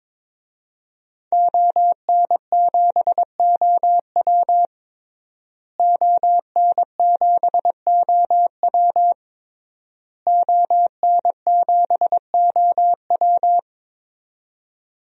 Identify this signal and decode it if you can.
Morse code, heard as a keyed tone, about 22 wpm, 700 Hz. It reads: ON7OW ON7OW ON7OW
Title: DXCC-100 Call Signs - 5 Characters - Repeated 3x - 22wpm